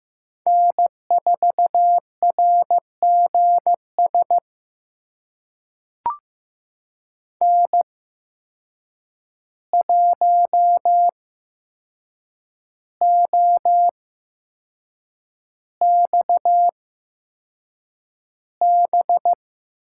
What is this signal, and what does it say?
Morse code, heard as a keyed tone, about 15 wpm, spaced as for 6 wpm, 700 Hz. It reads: N4RGS N1OXB
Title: USA Call Signs - 5 Characters - 8x Character Spacing - 15wpm